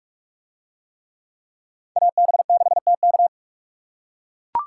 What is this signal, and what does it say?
Morse code, heard as a keyed tone, about 44 wpm, 700 Hz. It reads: AB6TK